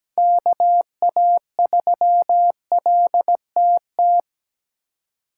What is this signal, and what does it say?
Morse code, heard as a keyed tone, about 20 wpm, 700 Hz. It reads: KA3LTT
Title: Speed Racing - USA 2x3 Call Signs 17wpm